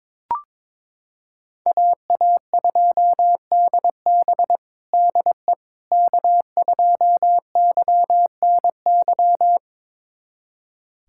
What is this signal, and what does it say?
Morse code, heard as a keyed tone, about 22 wpm, 700 Hz. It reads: AA2DB DE K2YNY
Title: USA Callsign DE Callsign - 5 Characters - 22wpm